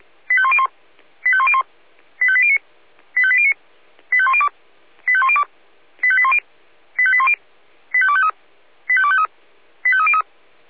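Five-tone selective call (ZVEI), heard as quick runs of five tones, tones of about 70 ms. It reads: 85101 85101 85809 85809 85102 85102 85810 85810 85253 85253 85303
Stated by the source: ZVEI five-tone call received on SDR.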